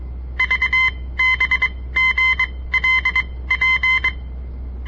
Morse code, heard as a keyed tone, about 22 wpm, spaced as for 17 wpm, 2050 Hz. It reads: VBGLP
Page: On the air Morse code practice from the Southern Montana Amatuer Radio Association; all code, no read back